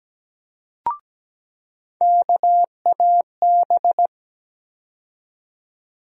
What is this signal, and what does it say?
Morse code, heard as a keyed tone, about 17 wpm, 700 Hz. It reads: KAB
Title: USA Call Sign Postfixes - 3 Letters 17wpm